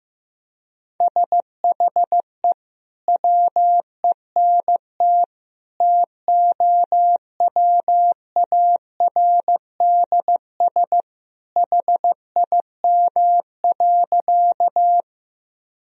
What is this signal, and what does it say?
Morse code, heard as a keyed tone, about 15 wpm, 700 Hz. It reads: SHE WENT TOWARDS HIM.
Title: Sentences from Top 1000 Words - Review Part 2 - 15wpm